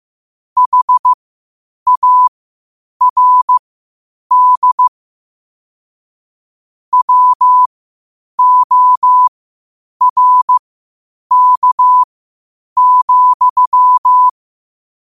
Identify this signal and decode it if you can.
Morse code, heard as a keyed tone, about 15 wpm, spaced as for 8 wpm, 1000 Hz. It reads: HARD WORK,
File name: qotdfiles / 8wpm